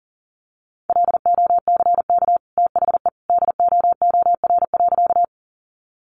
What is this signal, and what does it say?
Morse code, heard as a keyed tone, about 37 wpm, 700 Hz. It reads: LOCK THE DOOR.
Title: Sentences from Top 1300 Words 40wpm